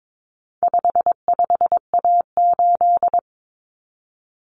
Morse code, heard as a keyed tone, about 22 wpm, 700 Hz. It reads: 55A8
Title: Common Cut Numbers - 4-Digit 22wpm